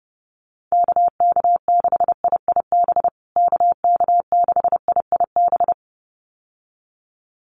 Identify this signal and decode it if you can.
Morse code, heard as a keyed tone, about 30 wpm, 700 Hz. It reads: KK6IIB KK6IIB
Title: USA Call Signs Repeated 2x 30wpm